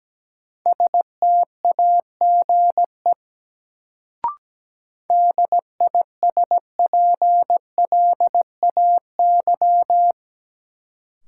Morse code, heard as a keyed tone, about 17 wpm, 700 Hz. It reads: STAGE DISPLAY
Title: Sentences from Top 1200 Words - New Word Drill Alt - 17wpm